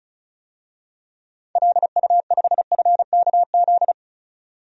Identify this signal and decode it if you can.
Morse code, heard as a keyed tone, about 36 wpm, 700 Hz. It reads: LU5FKZ
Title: DXCC-100 Call Signs - 6 Characters - 35wpm